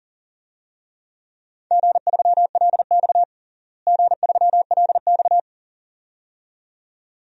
Morse code, heard as a keyed tone, about 40 wpm, 700 Hz. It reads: G3LX G3LX
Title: DXCC-100 Call Signs - 4 Characters - Repeated 2x - 40wpm